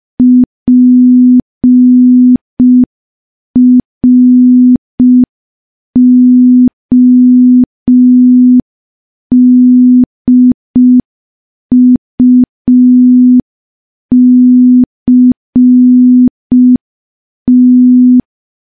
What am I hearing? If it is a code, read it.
Morse code, heard as a keyed tone, about 5 wpm, 250 Hz. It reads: PRODUCT
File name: morse / morse-training-05WPM